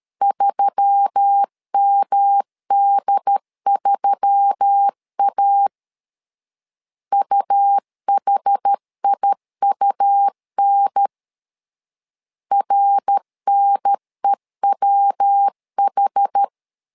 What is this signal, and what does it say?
Morse code, heard as a keyed tone, about 13 wpm, 800 Hz. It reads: 3MD3A UHIUN RNEWH
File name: LEZIONI / 4 Q1S3 12wpm 2023